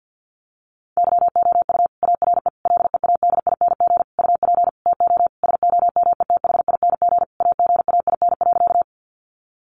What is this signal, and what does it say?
Morse code, heard as a keyed tone, about 50 wpm, 700 Hz. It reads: YOU ARE LEADING UP TO SOMETHING AGAIN.
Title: Sentences from Top 1100 Words 50wpm